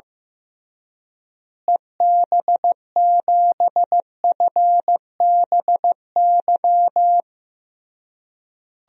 Morse code, heard as a keyed tone, about 15 wpm, 700 Hz. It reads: EB7FBY
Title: DXCC-100 Call Signs - 6 Characters - Repeated 3x - 15wpm